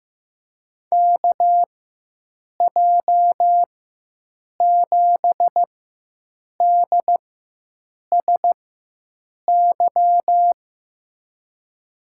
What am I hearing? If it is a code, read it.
Morse code, heard as a keyed tone, about 15 wpm, spaced as for 7 wpm, 700 Hz. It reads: KJ7DSY
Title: USA Call Signs - 6 Characters - 4x Character Spacing - 15wpm